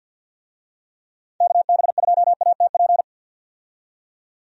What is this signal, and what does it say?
Morse code, heard as a keyed tone, about 52 wpm, 700 Hz. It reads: KB2ATP
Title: USA Call Signs - 6 Characters - 50wpm